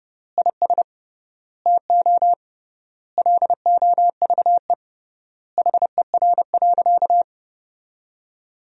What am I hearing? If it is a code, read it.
Morse code, heard as a keyed tone, about 30 wpm, 700 Hz. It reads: IS TO LOVE HER.
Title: 3x Word-Spacing - Sentences from Top 400 Words - 30wpm